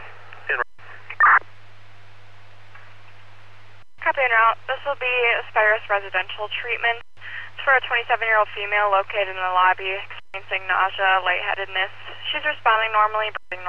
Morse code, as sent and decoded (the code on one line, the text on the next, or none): none